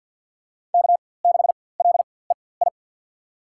Morse code, heard as a keyed tone, about 48 wpm, spaced as for 23 wpm, 700 Hz. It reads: K6LEI
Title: USA Call Signs - 5 Characters - 4x Character Spacing - 50wpm